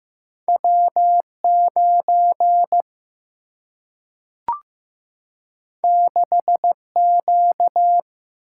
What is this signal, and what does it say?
Morse code, heard as a keyed tone, about 15 wpm, 700 Hz. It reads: W 9 6 Q
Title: Single-Letter Single-Number 15wpm